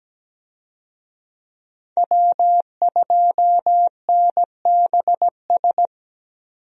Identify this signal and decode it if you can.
Morse code, heard as a keyed tone, about 17 wpm, 700 Hz. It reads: W2NBS
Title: USA Call Signs - 5 Characters - 17wpm